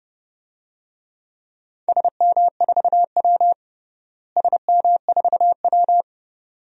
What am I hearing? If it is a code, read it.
Morse code, heard as a keyed tone, about 30 wpm, 700 Hz. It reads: SM4W SM4W
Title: DXCC-100 Call Signs - 4 Characters - Repeated 2x - 30wpm